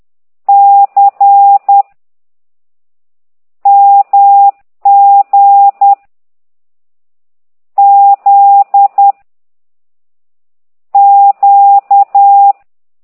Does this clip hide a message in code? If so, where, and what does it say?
Yes: Morse code from 0.5 to 12.5 s, heard as a keyed tone, about 10 wpm, 800 Hz. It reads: C MG Z Q